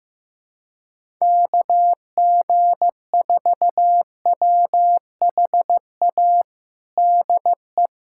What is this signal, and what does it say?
Morse code, heard as a keyed tone, about 15 wpm, 700 Hz. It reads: KG4WHA DE
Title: USA Callsign DE Callsign - 6 Characters - 15wpm